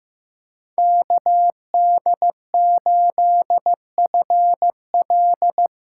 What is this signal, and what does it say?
Morse code, heard as a keyed tone, about 16 wpm, 700 Hz. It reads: KD8FL
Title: USA Callsign DE Callsign - 5 Characters - 15wpm